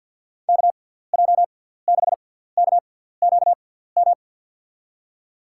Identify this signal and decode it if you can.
Morse code, heard as a keyed tone, about 52 wpm, spaced as for 17 wpm, 700 Hz. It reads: KJ6KQM
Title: USA Call Signs - 6 Characters - 6x Character Spacing - 50wpm